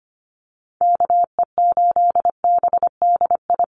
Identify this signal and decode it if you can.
Morse code, heard as a keyed tone, about 25 wpm, 700 Hz. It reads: KE8BDI
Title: USA Call Signs 25wpm